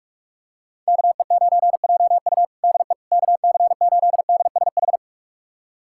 Morse code, heard as a keyed tone, about 42 wpm, 700 Hz. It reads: KE9JU DE KC8DSH
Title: USA Callsign DE Callsign 45wpm